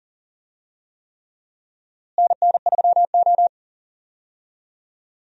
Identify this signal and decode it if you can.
Morse code, heard as a keyed tone, about 40 wpm, 700 Hz. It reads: NN3O
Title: USA Call Signs - 4 Characters - Repeated 3x - 40wpm